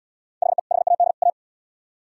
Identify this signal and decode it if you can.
Morse code, heard as a keyed tone, about 72 wpm, 700 Hz. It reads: HE HAD A